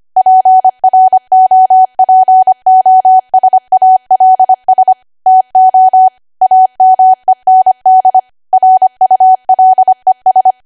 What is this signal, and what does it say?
Morse code, heard as a keyed tone, about 25 wpm, 750 Hz. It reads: PROPOSALS TO AMEND RULES
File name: cwa / W1AW-Level3 / 25WPM